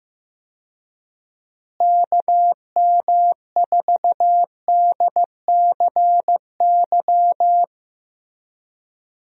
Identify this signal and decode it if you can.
Morse code, heard as a keyed tone, about 15 wpm, 700 Hz. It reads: KM4DCY
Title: USA Call Signs - 6 Characters - Repeated 2x - 15wpm